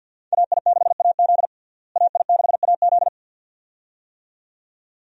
Morse code, heard as a keyed tone, about 53 wpm, 700 Hz. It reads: AI6AZ AI6AZ
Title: USA Call Signs - 5 Characters - Repeated 2x - 50wpm